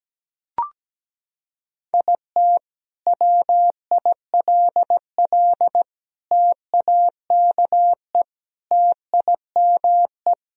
Morse code, heard as a keyed tone, about 17 wpm, 700 Hz. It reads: IT WILL TAKE TIME
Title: Sentences from Top 1000 Words - Review Part 1 - 17wpm